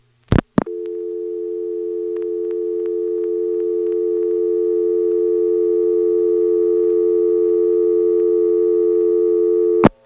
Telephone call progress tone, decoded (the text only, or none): dial tone